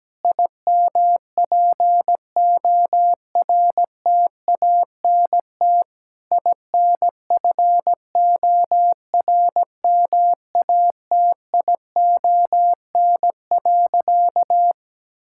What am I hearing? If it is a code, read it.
Morse code, heard as a keyed tone, about 17 wpm, 700 Hz. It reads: IMPORTANT INFORMATION.
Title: Sentences from Top 1000 Words - Review Part 6 - 17wpm